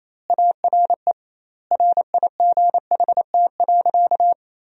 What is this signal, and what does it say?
Morse code, heard as a keyed tone, about 28 wpm, 700 Hz. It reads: ARE RIGHT.
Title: 2x Word-Spacing - Sentences from Top 500 Words - 28wpm